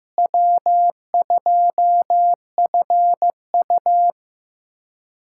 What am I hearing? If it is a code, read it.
Morse code, heard as a keyed tone, about 15 wpm, 700 Hz. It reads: W2FU
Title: USA Callsign DE Callsign - 4 Characters - 15wpm